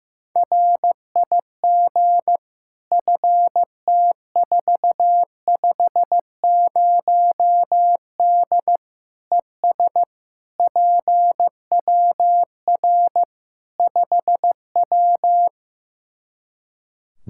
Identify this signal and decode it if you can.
Morse code, heard as a keyed tone, about 15 wpm, 700 Hz. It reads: RIG FT450D ES PWR 5W
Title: Practice QSO - Exchange 2 - Info - 15wpm